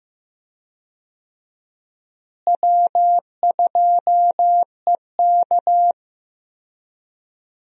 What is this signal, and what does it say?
Morse code, heard as a keyed tone, about 15 wpm, 700 Hz. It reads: W2EK
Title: USA Call Signs - 4 Characters - Repeated 2x - 15wpm